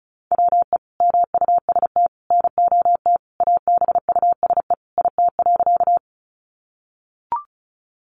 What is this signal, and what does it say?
Morse code, heard as a keyed tone, about 34 wpm, 700 Hz. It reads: WE MUST NOT ABUSE IT.
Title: Sentences from Top 1300 Words 35wpm